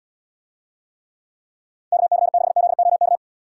Morse code, heard as a keyed tone, about 79 wpm, 700 Hz. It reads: KC6ZOK